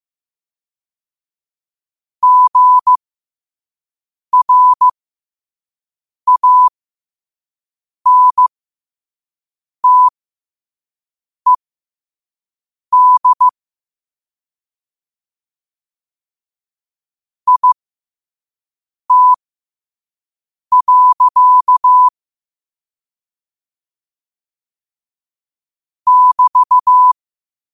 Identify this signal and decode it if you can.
Morse code, heard as a keyed tone, about 15 wpm, spaced as for 5 wpm, 1000 Hz. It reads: GRANTED IT. <BT>
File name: qotdfiles / 5wpm